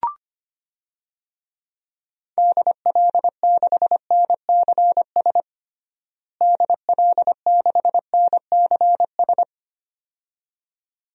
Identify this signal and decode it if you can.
Morse code, heard as a keyed tone, about 25 wpm, 700 Hz. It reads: DL6NCS DL6NCS
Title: DXCC-100 Call Signs - 6 Characters - Repeated 2x - 25wpm